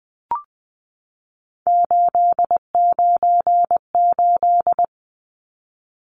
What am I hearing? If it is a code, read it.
Morse code, heard as a keyed tone, about 20 wpm, 700 Hz. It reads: 898
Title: Numbers - 3-Digit 20wpm